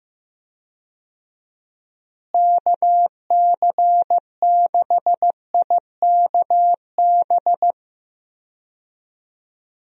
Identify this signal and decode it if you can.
Morse code, heard as a keyed tone, about 14 wpm, 700 Hz. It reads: KC6IKB
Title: USA Call Signs - 6 Characters - Repeated 3x - 15wpm